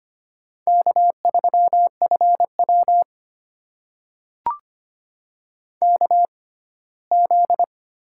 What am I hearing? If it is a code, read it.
Morse code, heard as a keyed tone, about 25 wpm, spaced as for 17 wpm, 700 Hz. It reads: K3FW KZ
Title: USA Call Signs - 4 Characters - 6x Character Spacing - 25wpm